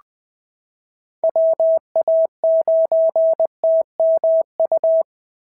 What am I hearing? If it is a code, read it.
Morse code, heard as a keyed tone, about 19 wpm, 650 Hz. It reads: WA9TMU
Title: POTA QSO Practice - 20wpm